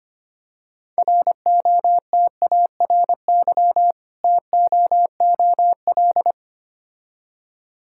Morse code, heard as a keyed tone, about 26 wpm, 700 Hz. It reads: ROTARY TOOL
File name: practice / 25wpm / Difficult Words - Speed-Racing with Hints - 25wpm